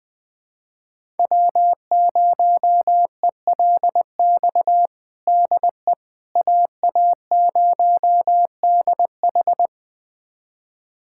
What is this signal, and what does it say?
Morse code, heard as a keyed tone, about 20 wpm, 700 Hz. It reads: W0ELX DE AA0DH
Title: USA Callsign DE Callsign - 5 Characters - 20wpm